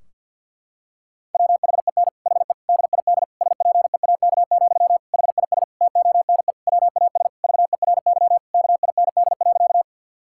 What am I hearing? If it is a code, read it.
Morse code, heard as a keyed tone, about 50 wpm, 700 Hz. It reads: WHEN HE DID SPEAK, HIS TONE WAS VERY KIND.
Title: Sentences from Top 1600 Words 50wpm